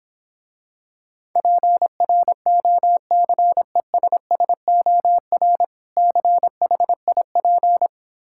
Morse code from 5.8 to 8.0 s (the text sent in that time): CHIP